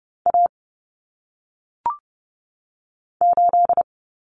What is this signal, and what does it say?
Morse code, heard as a keyed tone, about 30 wpm, 700 Hz. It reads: A 8
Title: Common Cut Numbers - 1-Digit 30wpm